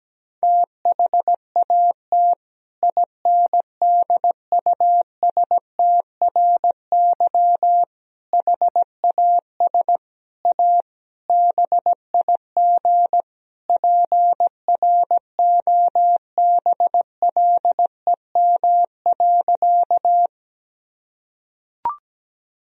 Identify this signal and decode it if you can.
Morse code, heard as a keyed tone, about 17 wpm, 700 Hz. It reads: THAT INDUSTRY HAS A BIG PROBLEM.
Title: Sentences from Top 1000 Words - Review Part 4 - 17wpm